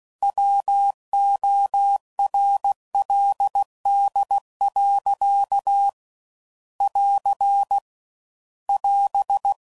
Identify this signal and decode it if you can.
Morse code, heard as a keyed tone, about 16 wpm, 800 Hz. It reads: WORLD. <AR> <AS>